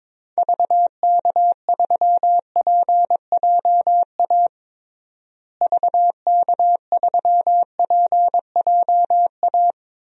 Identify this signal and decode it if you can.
Morse code, heard as a keyed tone, about 22 wpm, 700 Hz. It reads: VK3PJA VK3PJA
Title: DXCC-100 Call Signs - 6 Characters - Repeated 2x - 22wpm